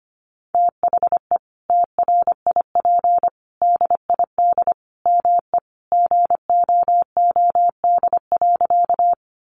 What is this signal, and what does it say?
Morse code, heard as a keyed tone, about 25 wpm, 700 Hz. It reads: THE TRIP DID ME GOOD.
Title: Sentences from Top 1300 Words 25wpm